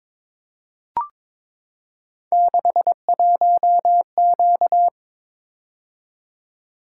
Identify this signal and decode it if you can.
Morse code, heard as a keyed tone, about 22 wpm, 700 Hz. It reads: 61Q
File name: practice / 22wpm / Three Letter-Number Combinations - 22wpm